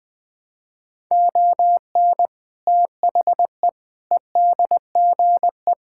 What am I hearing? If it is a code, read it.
Morse code, heard as a keyed tone, about 20 wpm, 700 Hz. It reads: ON THE EDGE
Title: Sentences with 3-Word Phrases - 20wpm